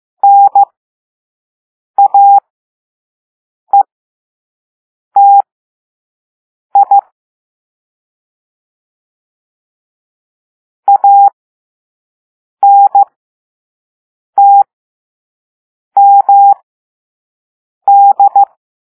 Morse code, heard as a keyed tone, about 15 wpm, spaced as for 5 wpm, 800 Hz. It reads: NAETI ANTMD
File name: Morse / Level3-All-B